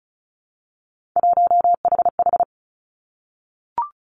Morse code, heard as a keyed tone, about 36 wpm, 700 Hz. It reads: 1HH
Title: Three Challenging Letter-Number Combinations 35wpm